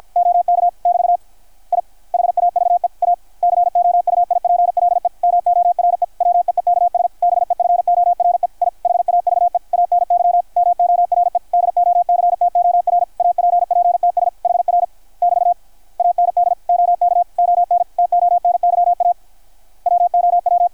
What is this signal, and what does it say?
Morse code, heard as a keyed tone, about 53 wpm, 660 Hz, with radio noise. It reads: OK <BT> I HAVE A COUIPLE MORE WEEKS BEFORE I HAVE ANY MORE DOCTOR APPTS HR <BT> AND OK ON TONYA WOW
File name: archived / CRSnet